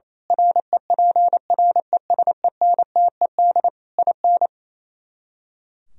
Morse code, heard as a keyed tone, about 26 wpm, 700 Hz. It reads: REPRESENTED IN
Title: Speed-Racing - Sets of 5 Words 28wpm